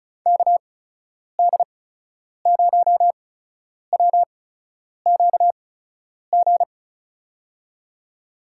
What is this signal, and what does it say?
Morse code, heard as a keyed tone, about 36 wpm, spaced as for 10 wpm, 700 Hz. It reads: KD0WQG
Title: USA Call Signs - 6 Characters - 8x Character Spacing - 35wpm